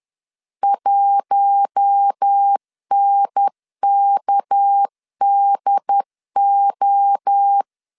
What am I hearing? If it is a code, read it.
Morse code, heard as a keyed tone, about 10 wpm, 800 Hz. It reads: 1NKDO
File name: CW / DET / 1 6 1500C S B 10wpm 4137